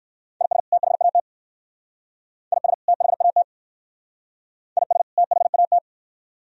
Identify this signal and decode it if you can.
Morse code, heard as a keyed tone, about 56 wpm, 700 Hz. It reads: IS THAT IS THAT IS THAT